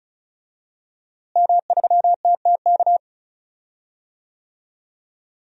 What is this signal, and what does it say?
Morse code, heard as a keyed tone, about 34 wpm, 700 Hz. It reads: M3TTK
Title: DXCC-100 Call Signs - 5 Characters - Repeated 2x - 35wpm